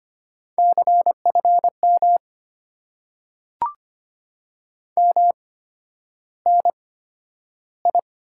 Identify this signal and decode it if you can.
Morse code, heard as a keyed tone, about 25 wpm, spaced as for 11 wpm, 700 Hz. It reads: CFM MNI
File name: Top 90 QSO Elements - 8x Character Spacing - 25wpm